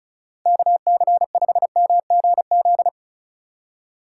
Morse code, heard as a keyed tone, about 44 wpm, 700 Hz. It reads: KC5MGZ